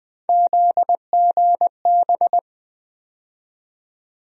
Morse code, heard as a keyed tone, about 21 wpm, 700 Hz. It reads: ZGB
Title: USA Call Sign Postfixes - 3 Letters 20wpm